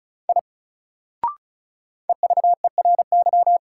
Morse code, heard as a keyed tone, about 36 wpm, 700 Hz. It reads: I EVERY
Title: Sentences from Top 200 Words - New Word Drill - 35wpm